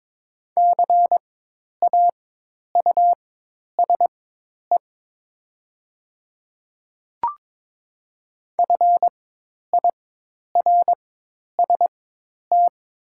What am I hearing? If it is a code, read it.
Morse code, heard as a keyed tone, about 22 wpm, spaced as for 10 wpm, 690 Hz. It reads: CAUSE FIRST
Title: 4x Character Spacing - Top 200 Words - 22wpm